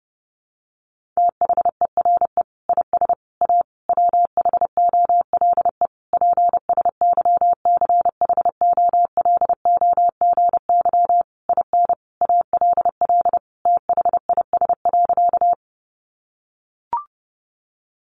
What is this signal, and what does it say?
Morse code, heard as a keyed tone, about 30 wpm, 700 Hz. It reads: THERE IS A WHOLE PSYCHOLOGY IN ALL THIS.
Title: Sentences from Top 1400 Words 30wpm